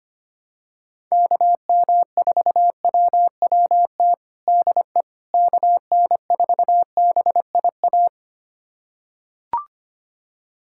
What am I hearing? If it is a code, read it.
Morse code, heard as a keyed tone, about 25 wpm, 700 Hz. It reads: KM4WWT DE KN4BIA
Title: USA Callsign DE Callsign 25wpm